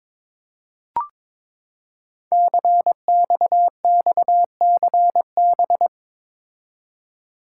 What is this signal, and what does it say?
Morse code, heard as a keyed tone, about 22 wpm, 700 Hz. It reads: CXXCB
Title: Code Groups - 5 Letters - 22wpm